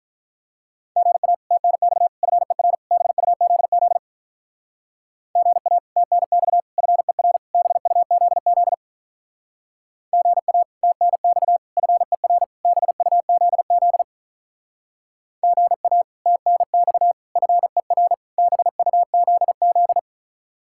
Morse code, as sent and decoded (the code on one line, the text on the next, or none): --. .- / - -. -..- / ..-. .-. / -... ..- --.. --.. / --. .- / - -. -..- / ..-. .-. / -... ..- --.. --.. / --. .- / - -. -..- / ..-. .-. / -... ..- --.. --.. / --. .- / - -. -..- / ..-. .-. / -... ..- --.. --..
GA TNX FER BUZZ GA TNX FER BUZZ GA TNX FER BUZZ GA TNX FER BUZZ